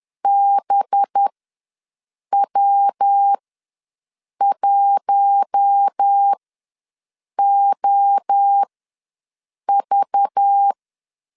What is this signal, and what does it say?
Morse code, heard as a keyed tone, about 10 wpm, spaced as for 6 wpm, 800 Hz. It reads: BW1OV
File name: LEZIONI / 7 GF9Z / 7 GF9Z ws 10wpm 3919